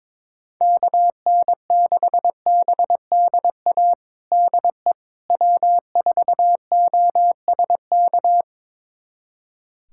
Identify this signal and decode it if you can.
Morse code, heard as a keyed tone, about 22 wpm, 700 Hz. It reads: KN6BDA DE W4OSK
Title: USA Callsign DE Callsign 22wpm